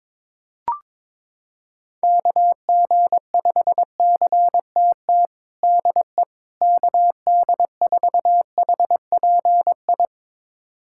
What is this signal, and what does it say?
Morse code, heard as a keyed tone, about 22 wpm, 700 Hz. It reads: KG5CTT DE KD4HPI
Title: USA Callsign DE Callsign 22wpm